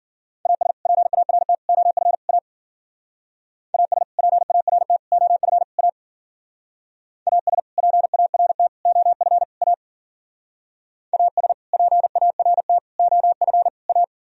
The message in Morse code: AS PART OF A AS PART OF A AS PART OF A AS PART OF A

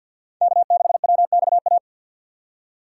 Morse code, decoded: K6WXA

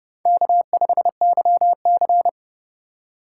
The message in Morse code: K5YC

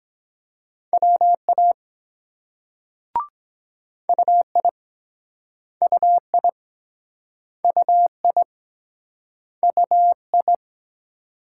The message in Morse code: WA UI UI UI UI

